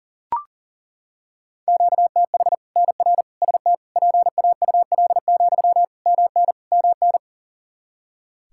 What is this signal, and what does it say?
Morse code, heard as a keyed tone, about 40 wpm, 700 Hz. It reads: QTH NR ST PAUL, MN MN
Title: Practice QSO - Exchange 1 - Beginning - 40wpm